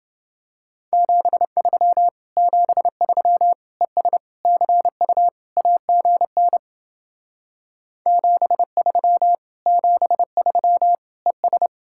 Morse code, heard as a keyed tone, about 29 wpm, 700 Hz. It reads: 73 73 ES CU AGN 73 73 ES